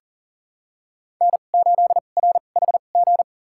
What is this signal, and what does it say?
Morse code, heard as a keyed tone, about 40 wpm, spaced as for 29 wpm, 700 Hz. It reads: N8RHG
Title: USA Call Signs - 5 Characters - 2x Character Spacing - 40wpm